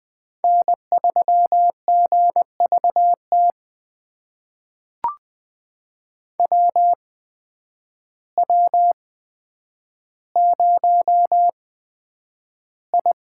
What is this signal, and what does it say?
Morse code, heard as a keyed tone, about 20 wpm, spaced as for 9 wpm, 700 Hz. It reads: N3GVT WW0I
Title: USA Call Signs - 8x Character Spacing - 20wpm